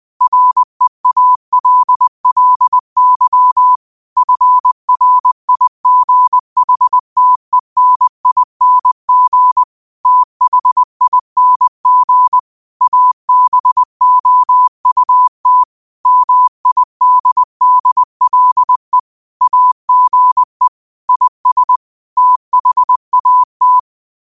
Morse code, heard as a keyed tone, about 20 wpm, 1000 Hz. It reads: REALLY FRIGHTENING THING ABOUT MIDDLE AGE IS THAT
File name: 20wpm